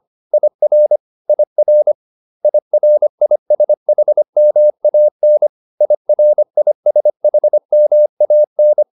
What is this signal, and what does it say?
Morse code, heard as a keyed tone, about 25 wpm, 600 Hz. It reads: IR IR IRISHMAN IRISHMAN